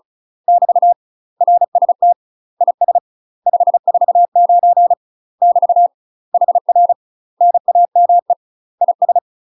Morse code, heard as a keyed tone, about 34 wpm, 700 Hz. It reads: <BT> RST IS 549 <BT> HR NAME IS
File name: UK MASTER QSOs TRAINING / 35WPM / UK MASTER QSOs 0005